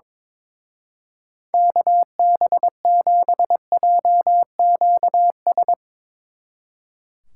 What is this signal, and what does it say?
Morse code, heard as a keyed tone, about 24 wpm, 700 Hz. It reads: KB7JQS
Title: Speed Racing - USA Call Signs 22wpm